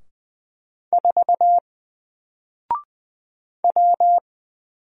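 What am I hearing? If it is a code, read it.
Morse code, heard as a keyed tone, about 20 wpm, 700 Hz. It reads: 4 W
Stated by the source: Single letter and number Morse drill at 20 wpm, from N4LCD.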